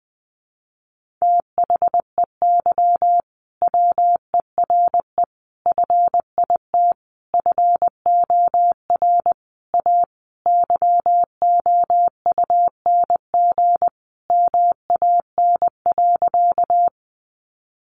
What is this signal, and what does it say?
Morse code, heard as a keyed tone, about 20 wpm, 700 Hz. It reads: THEY WERE FIT FOR A YOUNG MAN.
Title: Sentences from Top 800 Words 20wpm